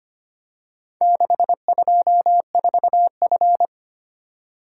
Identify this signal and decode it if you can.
Morse code, heard as a keyed tone, about 25 wpm, 700 Hz. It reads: 624F
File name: practice / 25wpm / Four Letter-Number Combinations - 25wpm